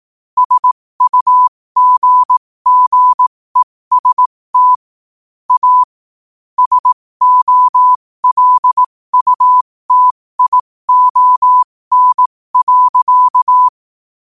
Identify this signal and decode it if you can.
Morse code, heard as a keyed tone, about 18 wpm, spaced as for 15 wpm, 1000 Hz. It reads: SUGGEST A SOLUTION.
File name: qotdfiles / 15wpm